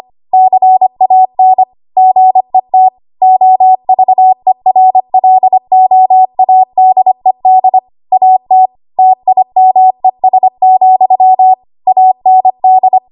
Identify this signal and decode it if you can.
Morse code, heard as a keyed tone, about 25 wpm, 750 Hz. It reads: CAN GET OVERLOADED AT TIMES, AND
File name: Morse / 25WPM